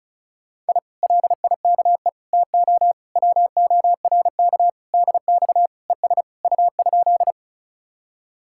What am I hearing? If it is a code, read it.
Morse code, heard as a keyed tone, about 48 wpm, spaced as for 42 wpm, 700 Hz. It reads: I LIKE TO WORK DX ES U?
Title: Speed-Racing Common QSO Phrases - 35wpm